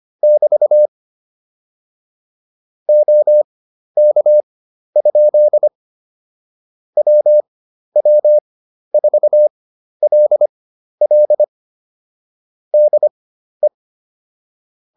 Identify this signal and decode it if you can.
Morse code, heard as a keyed tone, about 25 wpm, spaced as for 12 wpm, 600 Hz. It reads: <BT> OK? WW4LL DE